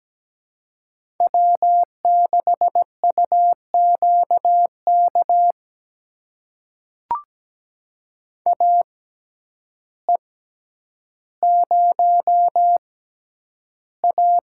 Morse code, heard as a keyed tone, about 17 wpm, spaced as for 9 wpm, 700 Hz. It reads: W6UQK AE0A
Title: USA Call Signs - 5 Characters - 6x Character Spacing - 17wpm